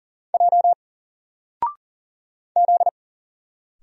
Morse code, heard as a keyed tone, about 40 wpm, 700 Hz. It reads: J Z
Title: Single Letter-Number 40wpm